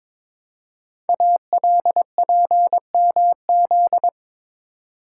Morse code, heard as a keyed tone, about 22 wpm, 700 Hz. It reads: ALPMZ
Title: Code Groups - 5 Letter:Numbers - 22wpm